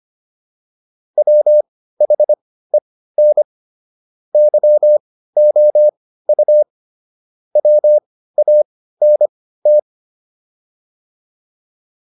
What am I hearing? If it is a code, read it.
Morse code, heard as a keyed tone, about 25 wpm, spaced as for 15 wpm, 600 Hz. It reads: WHEN YOU WANT
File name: PR305 15